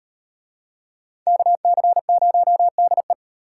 Morse code, heard as a keyed tone, about 33 wpm, 700 Hz. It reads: KC0DE